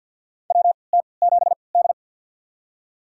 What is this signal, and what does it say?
Morse code, heard as a keyed tone, about 47 wpm, spaced as for 28 wpm, 700 Hz. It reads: WT7D
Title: USA Call Signs - 3x Character Spacing - 50wpm